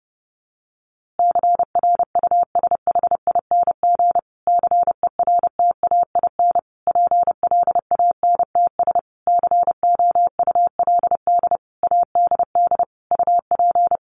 Morse code, heard as a keyed tone, about 30 wpm, 700 Hz. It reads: CRUSHING CERTAIN PLANTS COULD ADD UP
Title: A Sound Of Thunder 30wpm